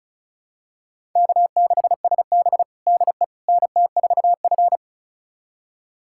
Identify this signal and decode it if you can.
Morse code, heard as a keyed tone, about 36 wpm, 700 Hz. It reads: K6SB DE NT4F
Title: USA Callsign DE Callsign - 4 Characters - 35wpm